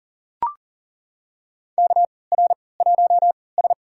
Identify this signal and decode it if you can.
Morse code, heard as a keyed tone, about 40 wpm, spaced as for 23 wpm, 700 Hz. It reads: KR1S
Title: USA Call Signs - 4 Characters - 3x Character Spacing - 40wpm